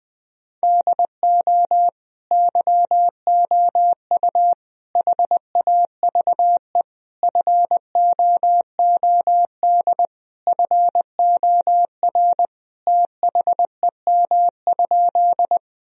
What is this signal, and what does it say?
Morse code, heard as a keyed tone, about 21 wpm, 700 Hz. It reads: DO YOU HAVE FOOD FOR THEM?
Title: Sentences from Top 1000 Words - Review Part 1 - 20wpm